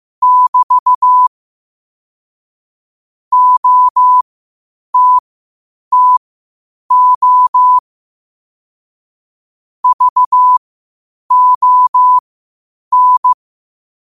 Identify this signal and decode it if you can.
Morse code, heard as a keyed tone, about 15 wpm, spaced as for 8 wpm, 1000 Hz. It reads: <BT> OTTO VON
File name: qotdfiles / 8wpm